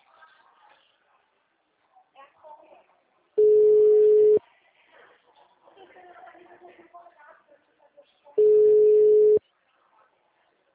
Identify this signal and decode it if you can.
Ringback tone (Europe, 425 Hz), with speech.